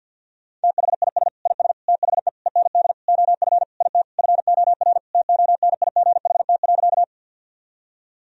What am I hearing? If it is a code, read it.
Morse code, heard as a keyed tone, about 51 wpm, 700 Hz. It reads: THIS IS THE END OF IT FOR TONIGHT.
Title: Sentences from Top 1000 Words - Review Part 3 - 50wpm